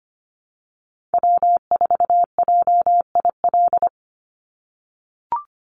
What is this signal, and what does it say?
Morse code, heard as a keyed tone, about 25 wpm, 700 Hz. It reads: W4JIL
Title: USA Call Signs 25wpm